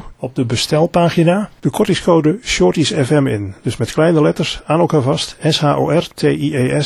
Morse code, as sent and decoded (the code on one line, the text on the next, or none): none